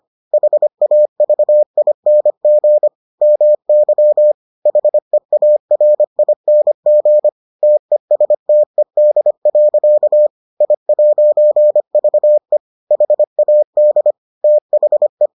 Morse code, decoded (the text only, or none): HAVING MY HEARING TESTED. I'VE HAD THE